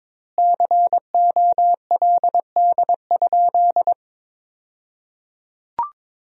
Morse code, heard as a keyed tone, about 22 wpm, 700 Hz. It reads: COLD?
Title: 2x Word-Spacing - Sentences from Top 400 Words - 22wpm